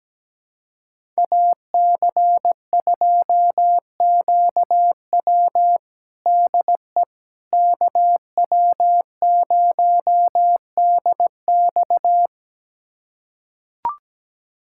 Morse code, heard as a keyed tone, about 17 wpm, 700 Hz. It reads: AC2QW DE KW0DX
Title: USA Callsign DE Callsign - 5 Characters - 17wpm